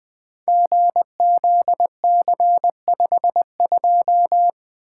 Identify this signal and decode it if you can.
Morse code, heard as a keyed tone, about 20 wpm, 700 Hz. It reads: GZC52
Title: Code Groups - 5 Letter:Numbers - 20wpm